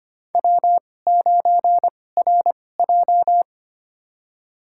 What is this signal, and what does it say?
Morse code, heard as a keyed tone, about 25 wpm, spaced as for 18 wpm, 700 Hz. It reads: W9RJ
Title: USA Call Signs - 4 Characters - 2x Character Spacing - 25wpm